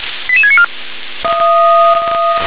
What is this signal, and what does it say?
Five-tone selective call (ZVEI), heard as quick runs of five tones, tones of about 70 ms. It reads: 88694